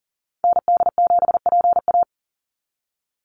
Morse code, heard as a keyed tone, about 40 wpm, 700 Hz. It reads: ND7PA
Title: NA SOTA Activator Callsigns 40wpm